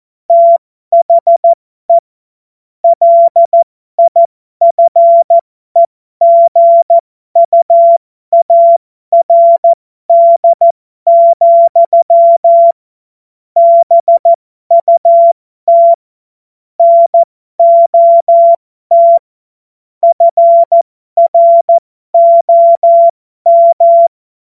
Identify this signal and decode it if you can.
Morse code, heard as a keyed tone, about 14 wpm, spaced as for 12 wpm, 680 Hz. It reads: THE LIFEGUARD, BUT NOT FROM
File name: Morsekurs / Witze